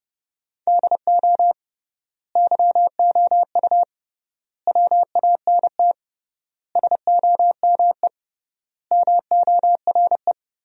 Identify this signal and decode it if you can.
Morse code, heard as a keyed tone, about 30 wpm, 700 Hz. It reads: DO YOU WANT SOME MORE
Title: 3x Word-Spacing - Sentences from Top 300 Words - 30wpm